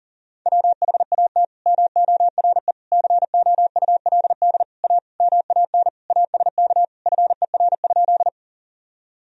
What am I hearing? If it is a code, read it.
Morse code, heard as a keyed tone, about 40 wpm, 700 Hz. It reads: WHAT MORE COULD A MAN ASK FER?
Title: Sentences from Top 200 Words and Abbreviations 40wpm